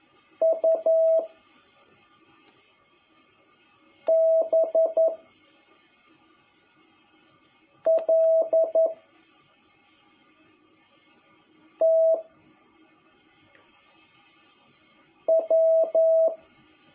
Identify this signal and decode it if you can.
Morse code, heard as a keyed tone, about 11 wpm, 650 Hz. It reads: U B L T W
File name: jh8cbh / 2021cw